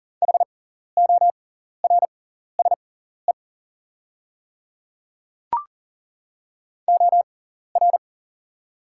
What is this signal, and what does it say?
Morse code, heard as a keyed tone, about 42 wpm, spaced as for 14 wpm, 690 Hz. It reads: HORSE OR
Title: 6x Character Spacing - Top 300 Words - 40wpm